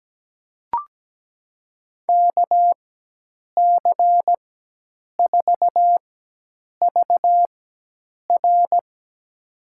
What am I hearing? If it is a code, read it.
Morse code, heard as a keyed tone, about 17 wpm, spaced as for 8 wpm, 700 Hz. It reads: KC4VR
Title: USA Call Signs - 5 Characters - 4x Character Spacing - 17wpm